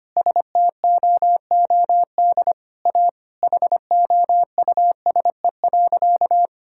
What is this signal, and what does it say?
Morse code, heard as a keyed tone, about 25 wpm, 700 Hz. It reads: STOOD A HOUSE.
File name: practice / 25wpm / Sentences with 4-Word Phrases - 25wpm